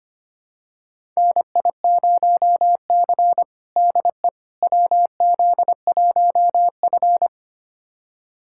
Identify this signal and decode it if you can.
Morse code, heard as a keyed tone, about 25 wpm, 700 Hz. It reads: NI0C DE WZ1F
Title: USA Callsign DE Callsign - 4 Characters - 25wpm